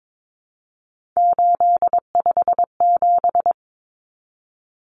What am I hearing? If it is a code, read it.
Morse code, heard as a keyed tone, about 22 wpm, 700 Hz. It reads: 857
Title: Common Cut Numbers - 3-Digit 22wpm